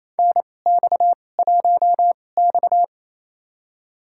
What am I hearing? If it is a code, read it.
Morse code, heard as a keyed tone, about 28 wpm, spaced as for 20 wpm, 700 Hz. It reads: NX1X
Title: USA Call Signs - 2x Character Spacing - 28wpm